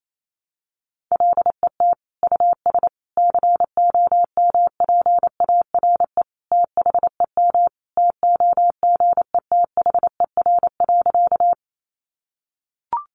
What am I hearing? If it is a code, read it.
Morse code, heard as a keyed tone, about 28 wpm, 700 Hz. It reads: LET US COMPARE THEM TOGETHER.
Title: Sentences from Top 1300 Words 28wpm